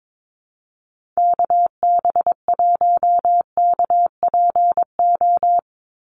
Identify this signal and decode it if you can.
Morse code, heard as a keyed tone, about 22 wpm, 700 Hz. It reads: KB1KPO
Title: USA Call Signs 22wpm